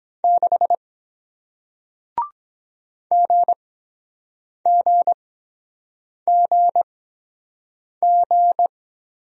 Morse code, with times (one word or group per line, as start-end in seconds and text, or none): 0.2-0.8 6
3.1-8.7 GGGG